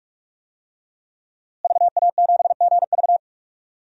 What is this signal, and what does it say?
Morse code, heard as a keyed tone, about 46 wpm, 700 Hz. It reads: VA7GV
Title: DXCC-100 Call Signs - 5 Characters - Repeated 2x - 45wpm